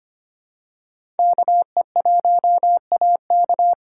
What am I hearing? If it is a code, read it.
Morse code, heard as a keyed tone, about 25 wpm, 700 Hz. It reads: KE1AK
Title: USA Callsign DE Callsign - 5 Characters - 25wpm